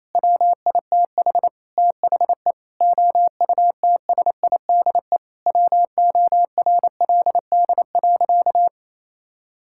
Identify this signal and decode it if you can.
Morse code, heard as a keyed tone, about 28 wpm, 700 Hz. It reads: WITH THE OUTSIDE WORLD.
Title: Sentences from Top 1000 Words - Review Part 6 - 28wpm